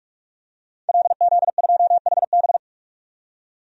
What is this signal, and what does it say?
Morse code, heard as a keyed tone, about 42 wpm, 700 Hz. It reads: LZ2HB
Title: DXCC-100 Call Signs - 5 Characters - 45wpm